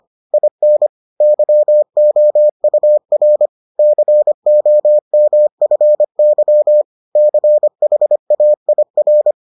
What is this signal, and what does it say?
Morse code, heard as a keyed tone, about 25 wpm, 600 Hz. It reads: IN YOUR COMFY CHAIR